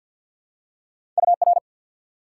Morse code, heard as a keyed tone, about 53 wpm, 700 Hz. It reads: UR
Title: Top 180 QSO Elements 50wpm